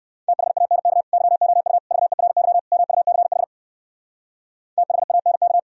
Morse code, heard as a keyed tone, about 68 wpm, 700 Hz. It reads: THANK YOU VERY MUCH THANK